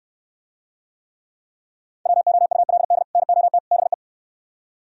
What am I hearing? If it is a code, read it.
Morse code, heard as a keyed tone, about 63 wpm, 700 Hz. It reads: WOULD NOT BE